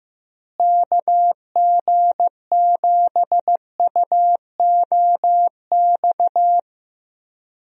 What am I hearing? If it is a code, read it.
Morse code, heard as a keyed tone, about 15 wpm, 700 Hz. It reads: KG7UOX
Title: USA Call Signs - 6 Characters - Repeated 2x - 15wpm